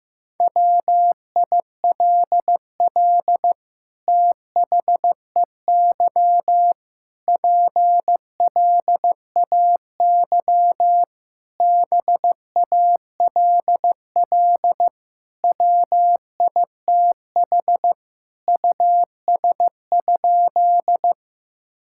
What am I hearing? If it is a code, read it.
Morse code, heard as a keyed tone, about 15 wpm, 700 Hz. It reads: WILL THEY PLAY BALL WITH US?
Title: Sentences from Top 1000 Words - Review Part 6 - 15wpm